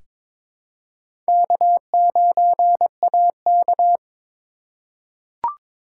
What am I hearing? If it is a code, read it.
Morse code, heard as a keyed tone, about 22 wpm, 700 Hz. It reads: K9AK
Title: USA Call Signs - 2x Character Spacing - 22wpm